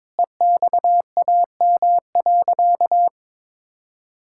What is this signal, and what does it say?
Morse code, heard as a keyed tone, about 22 wpm, 700 Hz. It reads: EXAM.
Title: Sentences from Top 1000 Words - Review Part 5 - 22wpm